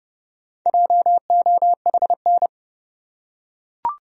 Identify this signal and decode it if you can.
Morse code, heard as a keyed tone, about 29 wpm, 700 Hz. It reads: JOHN
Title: USA Common Names - Max 5 Letters - 30wpm